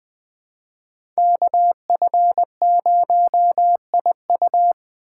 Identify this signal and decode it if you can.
Morse code, heard as a keyed tone, about 20 wpm, 700 Hz. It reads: KF0IU
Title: USA Call Signs - 5 Characters - 2x Character Spacing - 20wpm